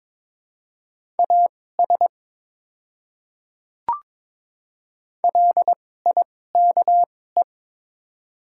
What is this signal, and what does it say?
Morse code, heard as a keyed tone, about 21 wpm, spaced as for 16 wpm, 700 Hz. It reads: AS LIKE
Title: 2x Character Spacing - Top 100 Words - 22wpm